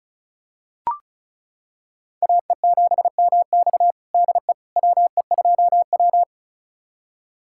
Morse code, heard as a keyed tone, about 36 wpm, 700 Hz. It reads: AE7MX DE WE2W
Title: USA Callsign DE Callsign 35wpm